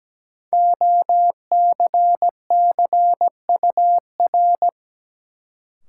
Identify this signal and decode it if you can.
Morse code, heard as a keyed tone, about 17 wpm, 700 Hz. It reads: OCCUR
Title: Sentences from Top 900 Words - New Word Drill - 17wpm